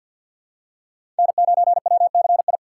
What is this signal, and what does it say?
Morse code, heard as a keyed tone, about 49 wpm, 700 Hz. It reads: N9WCI
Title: USA Call Signs - 5 Characters - 6x Character Spacing - 50wpm